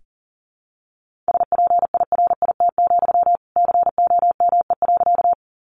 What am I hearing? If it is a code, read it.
Morse code, heard as a keyed tone, about 37 wpm, 700 Hz. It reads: SPIRIT, COME.
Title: Sentences from Top 1400 Words 40wpm